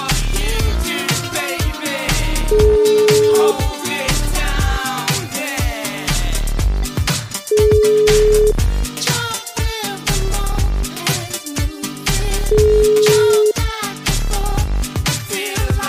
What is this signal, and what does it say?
Ringback tone (Europe, 425 Hz), with noise.